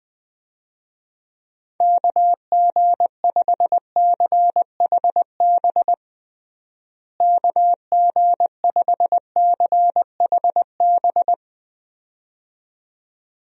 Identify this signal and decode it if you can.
Morse code, heard as a keyed tone, about 21 wpm, 700 Hz. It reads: KG5CHB KG5CHB
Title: USA Call Signs - 6 Characters - Repeated 2x - 20wpm